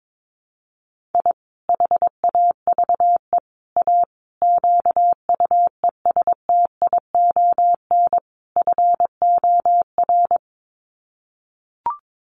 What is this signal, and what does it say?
Morse code, heard as a keyed tone, about 22 wpm, 700 Hz. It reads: I HAVE A QUESTION FOR
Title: Sets of 5 Words Encore2 22wpm